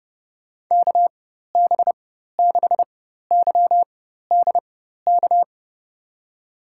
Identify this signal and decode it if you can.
Morse code, heard as a keyed tone, about 30 wpm, spaced as for 14 wpm, 700 Hz. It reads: KB6YDK
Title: USA Call Signs - 6 Characters - 4x Character Spacing - 30wpm